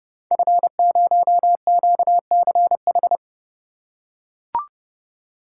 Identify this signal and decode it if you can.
Morse code, heard as a keyed tone, about 30 wpm, 700 Hz. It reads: F0QCH
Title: Code Groups - 5 Letter:Numbers - 30wpm